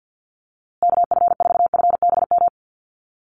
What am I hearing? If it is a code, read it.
Morse code, heard as a keyed tone, about 48 wpm, 700 Hz. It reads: KF4FBM
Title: USA Call Signs Repeated 2x 50wpm